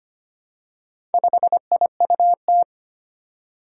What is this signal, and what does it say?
Morse code, heard as a keyed tone, about 25 wpm, 700 Hz. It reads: 5IUT
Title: Four Challenging Letter-Number Combinations 25wpm